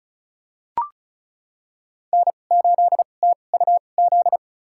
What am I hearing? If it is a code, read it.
Morse code, heard as a keyed tone, about 36 wpm, spaced as for 25 wpm, 700 Hz. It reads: N8TUZ
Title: USA Call Signs - 2x Character Spacing - 35wpm